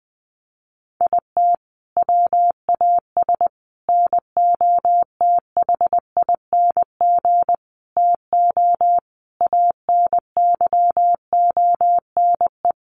Morse code, heard as a keyed tone, about 20 wpm, 700 Hz. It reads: IT WAS NOTHING TO ANYONE
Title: Sentences from Top 1300 Words 20wpm